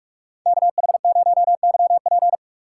Morse code, heard as a keyed tone, about 46 wpm, 700 Hz. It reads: KH0YP
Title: USA Call Signs - 5 Characters - 3x Character Spacing - 45wpm